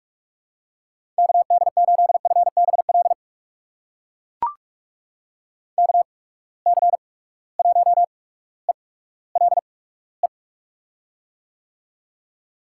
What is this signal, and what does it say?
Morse code, heard as a keyed tone, about 44 wpm, spaced as for 19 wpm, 700 Hz. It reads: KD8FBL KC1ELE
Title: USA Call Signs - 6 Characters - 8x Character Spacing - 45wpm